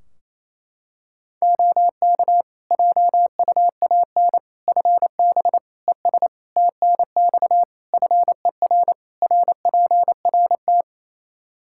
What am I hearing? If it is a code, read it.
Morse code, heard as a keyed tone, about 28 wpm, 700 Hz. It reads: OK JUAN FB ES TNX FER RPRT
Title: K1USN SST QSO Practice 28wpm